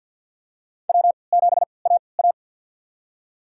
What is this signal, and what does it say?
Morse code, heard as a keyed tone, about 52 wpm, spaced as for 29 wpm, 700 Hz. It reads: W7AA
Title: USA Call Signs - 3x Character Spacing - 50wpm